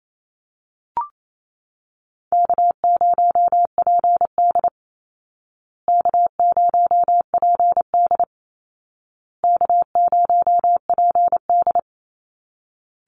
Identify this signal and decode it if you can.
Morse code, heard as a keyed tone, about 28 wpm, 700 Hz. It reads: K0PD K0PD K0PD
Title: Easy Call Signs Repeated 3x - 28wpm